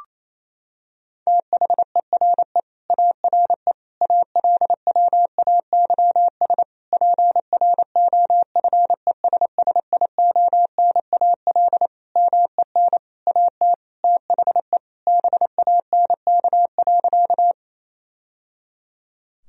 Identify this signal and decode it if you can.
Morse code, heard as a keyed tone, about 28 wpm, 700 Hz. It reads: THERE ARE ALWAYS PROFESSIONAL MEN AT THE BANK.
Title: Sentences from Top 1000 Words - Review Part 4 - 28wpm